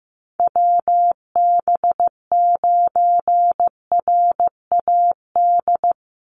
Morse code, heard as a keyed tone, about 15 wpm, 700 Hz. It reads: WB9RAD
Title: USA Call Signs 15wpm